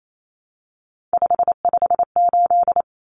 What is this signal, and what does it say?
Morse code, heard as a keyed tone, about 28 wpm, 700 Hz. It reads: 558
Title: Common Cut Numbers - 3-Digit 28wpm